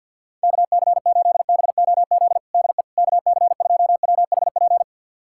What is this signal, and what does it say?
Morse code, heard as a keyed tone, about 49 wpm, 690 Hz. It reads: KC8BOZ DE KC2WHP
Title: USA Callsign DE Callsign - 6 Characters - 50wpm